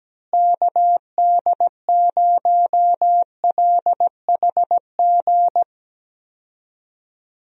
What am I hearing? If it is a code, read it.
Morse code, heard as a keyed tone, about 17 wpm, 700 Hz. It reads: KD0LHG
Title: USA Call Signs - 6 Characters - Repeated 2x - 17wpm